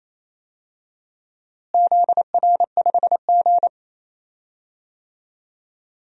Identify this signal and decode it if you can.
Morse code, heard as a keyed tone, about 28 wpm, 700 Hz. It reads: ZR5G
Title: DXCC-100 Call Signs - 4 Characters - Repeated 2x - 28wpm